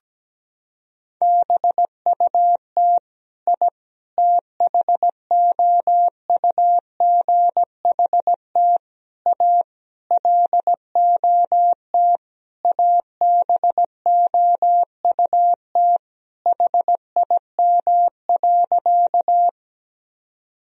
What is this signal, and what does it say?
Morse code, heard as a keyed tone, about 17 wpm, 700 Hz. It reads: BUT I THOUGHT A LOT ABOUT HIM.
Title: Sentences from Top 1000 Words - Review Part 3 - 17wpm